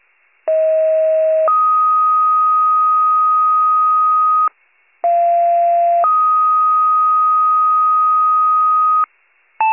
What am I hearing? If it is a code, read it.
Two-tone page, heard as a one-second tone, then a three-second tone, about 640 then 1250 Hz, about 690 then 1250 Hz. It is heard from 0.5 to 9.0 s.